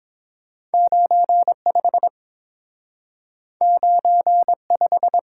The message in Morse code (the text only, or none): 95 95